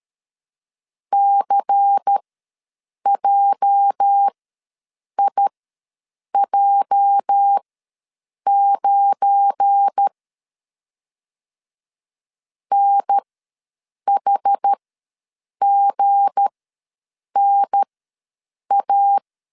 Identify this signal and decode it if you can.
Morse code, heard as a keyed tone, about 13 wpm, spaced as for 7 wpm, 800 Hz. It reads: CJIJ9 NHGNA
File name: LEZIONI / 8 YJ45 / 8 YJ ws 12wpm 3342